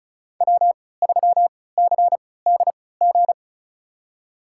Morse code, heard as a keyed tone, about 36 wpm, spaced as for 20 wpm, 690 Hz. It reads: W3CDG
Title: USA Call Signs - 5 Characters - 3x Character Spacing - 35wpm